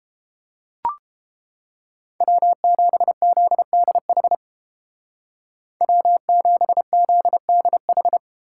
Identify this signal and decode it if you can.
Morse code, heard as a keyed tone, about 28 wpm, 700 Hz. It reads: W7ZDH W7ZDH